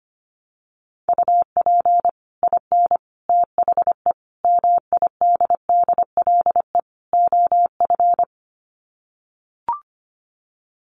Morse code, heard as a keyed tone, about 25 wpm, 700 Hz. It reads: UP IN THE MIDDLE OF
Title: Sets of 5 Words Encore 25wpm